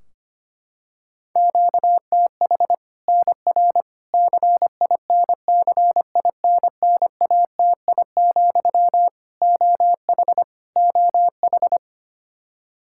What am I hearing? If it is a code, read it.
Morse code, heard as a keyed tone, about 25 wpm, 700 Hz. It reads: QTH NR CINCINNATI, OH OH
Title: Practice QSO - Exchange 1 - Beginning - 25wpm